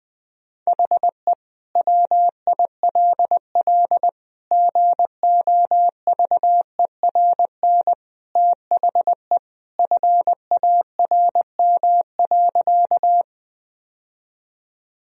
Morse code, heard as a keyed tone, about 20 wpm, 700 Hz. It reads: HE WILL GOVERN THE FARM.
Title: Sentences from Top 1000 Words - Review Part 2 - 20wpm